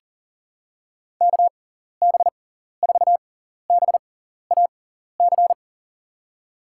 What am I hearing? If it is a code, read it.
Morse code, heard as a keyed tone, about 40 wpm, spaced as for 14 wpm, 700 Hz. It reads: KB4BAC
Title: USA Call Signs - 6 Characters - 6x Character Spacing - 40wpm